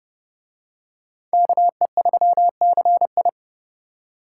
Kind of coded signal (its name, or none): Morse code